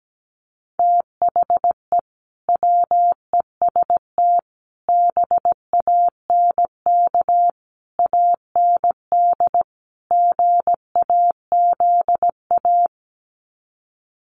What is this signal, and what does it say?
Morse code, heard as a keyed tone, about 17 wpm, 700 Hz. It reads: THE WEST BANK AND GAZA
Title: Sets of 5 Words 17wpm